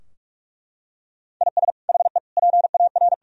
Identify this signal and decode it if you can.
Morse code, heard as a keyed tone, about 41 wpm, 700 Hz. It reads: IS HE PAR
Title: Sentences from Top 1000 Words - Review Part 4 - 45wpm